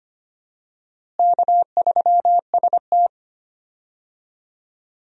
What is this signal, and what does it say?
Morse code, heard as a keyed tone, about 25 wpm, 700 Hz. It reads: K3ST
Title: USA Call Signs - 4 Characters - Repeated 2x - 25wpm